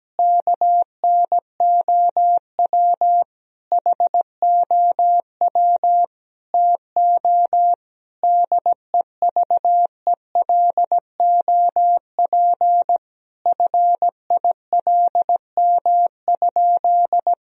Morse code, heard as a keyed tone, about 17 wpm, 700 Hz. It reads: KNOW HOW TO DEVELOP FILM?
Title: Sentences from Top 1000 Words - Review Part 3 - 17wpm